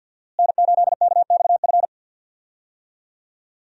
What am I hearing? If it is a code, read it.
Morse code, heard as a keyed tone, about 51 wpm, 700 Hz. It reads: N8KXF
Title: USA Call Signs - 5 Characters - Repeated 3x - 50wpm